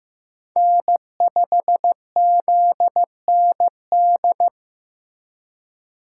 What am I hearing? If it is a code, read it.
Morse code, heard as a keyed tone, about 18 wpm, 700 Hz. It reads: N5ZND
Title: Speed Racing - USA 1x3 Call Signs 15wpm